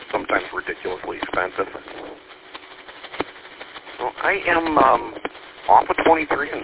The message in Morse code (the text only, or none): none